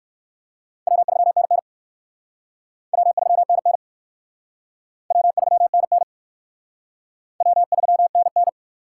Morse code, heard as a keyed tone, about 54 wpm, 700 Hz. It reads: W3NN W3NN W3NN W3NN
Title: Speed Racing - USA 1x2 Call Signs 45wpm